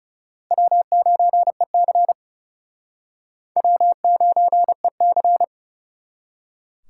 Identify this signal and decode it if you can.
Morse code, heard as a keyed tone, about 38 wpm, 700 Hz. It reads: W9EC W9EC